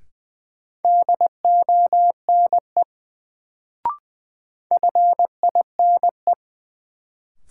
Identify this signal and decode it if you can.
Morse code, heard as a keyed tone, about 21 wpm, 700 Hz. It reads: DONE FINE
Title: Sentences from Top 500 Words - New Word Drill - 20wpm